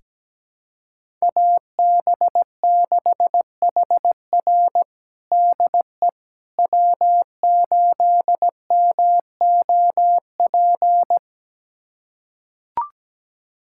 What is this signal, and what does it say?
Morse code, heard as a keyed tone, about 17 wpm, 700 Hz. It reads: AB6HR DE W8MOP
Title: USA Callsign DE Callsign - 5 Characters - 17wpm